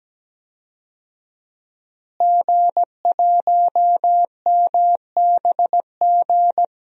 Morse code, heard as a keyed tone, about 17 wpm, 700 Hz. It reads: G1MBG
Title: DXCC-100 Call Signs - 5 Characters - Repeated 3x - 17wpm